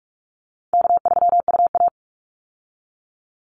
Morse code, heard as a keyed tone, about 44 wpm, 700 Hz. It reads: K3UA
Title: Easy Call Signs Repeated 2x - 45wpm